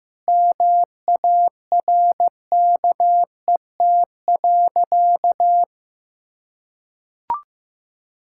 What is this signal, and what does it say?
Morse code, heard as a keyed tone, about 15 wpm, 700 Hz. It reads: MARKET.
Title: Sentences from Top 1600 Words 15wpm